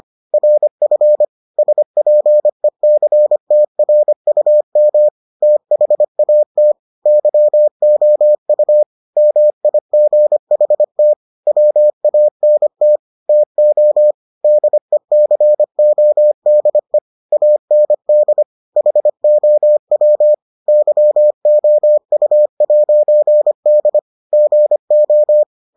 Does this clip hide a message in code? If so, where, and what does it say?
Yes: Morse code from 0.3 to 25.4 s, heard as a keyed tone, about 25 wpm, 600 Hz. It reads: RF SPECTRUM THAT YOU MIGHT WANT TO DECODE AND HOW YOU'D GO